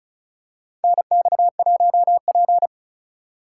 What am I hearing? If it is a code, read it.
Morse code, heard as a keyed tone, about 36 wpm, 700 Hz. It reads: NX1P
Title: USA Call Signs - 4 Characters - 2x Character Spacing - 35wpm